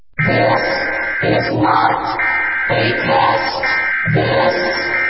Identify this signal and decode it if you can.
Emergency Alert System (SAME) tones, heard as harsh buzzing bursts. It is heard from 0.8 to 4.0 s.